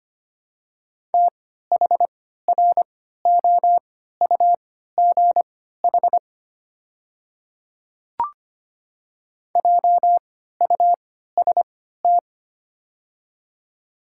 Morse code, heard as a keyed tone, about 25 wpm, spaced as for 14 wpm, 700 Hz. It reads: THROUGH JUST
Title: 3x Character Spacing - Top 300 Words - 25wpm